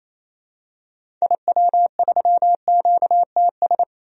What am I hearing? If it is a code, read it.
Morse code, heard as a keyed tone, about 27 wpm, 700 Hz. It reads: IW3QTS
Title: DXCC-100 Call Signs - 6 Characters - 28wpm